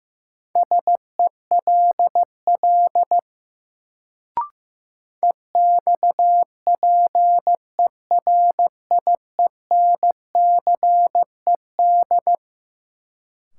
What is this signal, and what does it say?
Morse code, heard as a keyed tone, about 15 wpm, 700 Hz. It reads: SELL EXPERIENCED
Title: Sentences from Top 1100 Words - New Word Drill Alt - 15wpm